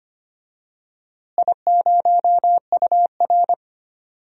Morse code, heard as a keyed tone, about 25 wpm, 700 Hz. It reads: I0UR
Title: DXCC-100 Call Signs - 4 Characters - Repeated 2x - 25wpm